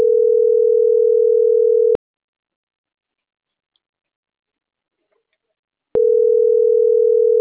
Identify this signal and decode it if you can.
Ringback tone (North America), 440 + 480 Hz.